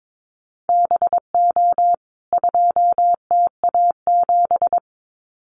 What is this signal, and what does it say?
Morse code, heard as a keyed tone, about 22 wpm, 700 Hz. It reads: BO 2TA7
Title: CWT Exchange 22wpm